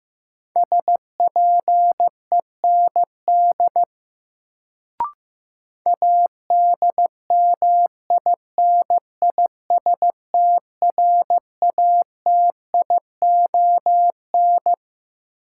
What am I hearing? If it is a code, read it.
Morse code, heard as a keyed tone, about 15 wpm, 700 Hz. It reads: SPEND ADMINISTRATION
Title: Sets of 5 Words - New Word Drill - 15wpm